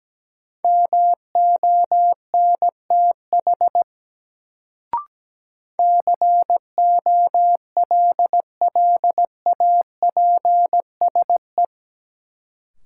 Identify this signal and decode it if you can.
Morse code, heard as a keyed tone, about 17 wpm, 700 Hz. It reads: MONTH COLLAPSE
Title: Sets of 5 Words Encore - New Word Drill - 17wpm